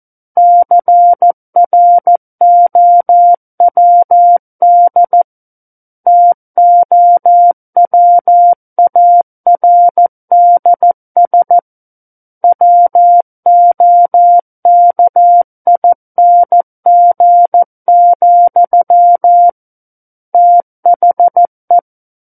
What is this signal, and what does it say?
Morse code, heard as a keyed tone, about 14 wpm, 700 Hz. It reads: CROWD TOWARDS WOKING, THE
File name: Book One / 05-Chapter 5 - 14 WPM